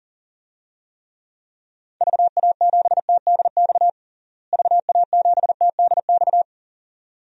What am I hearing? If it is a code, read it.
Morse code, heard as a keyed tone, about 40 wpm, 700 Hz. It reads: VA7TDX VA7TDX
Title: DXCC-100 Call Signs - 6 Characters - Repeated 2x - 40wpm